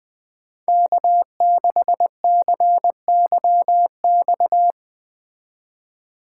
Morse code, heard as a keyed tone, about 21 wpm, 700 Hz. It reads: K6CYX
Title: Speed Racing - USA 1x3 Call Signs 17wpm